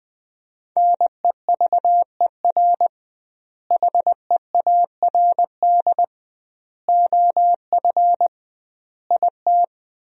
Morse code, heard as a keyed tone, about 20 wpm, 700 Hz. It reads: NEVER HEARD OF IT
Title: 2x Word-Spacing - Sentences from Top 400 Words - 20wpm